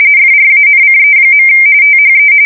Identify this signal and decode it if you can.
RTTY, about 45.45 baud. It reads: FRANCE. THE
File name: fsk170 / usnavy4